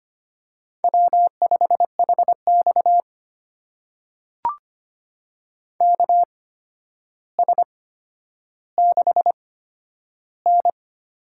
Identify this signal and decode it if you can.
Morse code, heard as a keyed tone, about 25 wpm, spaced as for 11 wpm, 700 Hz. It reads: W5HX KS6N
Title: USA Call Signs - 8x Character Spacing - 25wpm